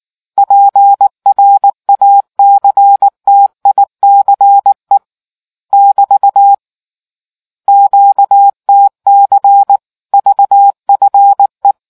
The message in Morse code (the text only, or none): PRACTICE <BT> QTC VFE